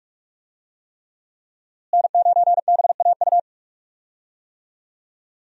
Morse code, heard as a keyed tone, about 41 wpm, 700 Hz. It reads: N9BAU
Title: USA Call Signs - 5 Characters - Repeated 3x - 45wpm